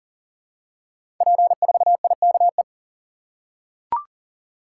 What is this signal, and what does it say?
Morse code, heard as a keyed tone, about 39 wpm, 700 Hz. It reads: P4IKE
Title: Code Groups - 5 Letter:Numbers - 40wpm